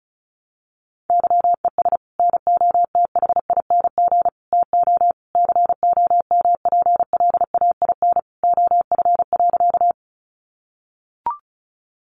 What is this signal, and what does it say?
Morse code, heard as a keyed tone, about 34 wpm, 700 Hz. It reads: YES NOTHING TO COMPLAIN OF.
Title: Sentences from Top 1300 Words 35wpm